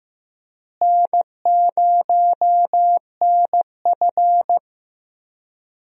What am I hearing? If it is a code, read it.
Morse code, heard as a keyed tone, about 16 wpm, 700 Hz. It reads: N0NF
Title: USA Call Signs - 4 Characters - Repeated 3x - 15wpm